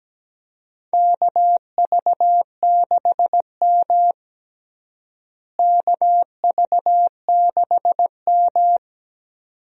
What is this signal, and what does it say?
Morse code, heard as a keyed tone, about 17 wpm, 700 Hz. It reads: KV6M KV6M
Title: USA Call Signs - 4 Characters - Repeated 2x - 17wpm